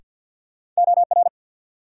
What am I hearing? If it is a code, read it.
Morse code, heard as a keyed tone, about 54 wpm, 700 Hz. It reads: OR
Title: Sentences from Top 100 Words - New Word Drill - 50wpm